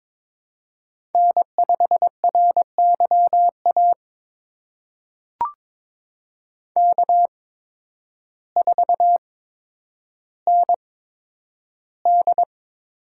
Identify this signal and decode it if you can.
Morse code, heard as a keyed tone, about 22 wpm, spaced as for 10 wpm, 700 Hz. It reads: N5RYA K4ND
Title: USA Call Signs - 8x Character Spacing - 22wpm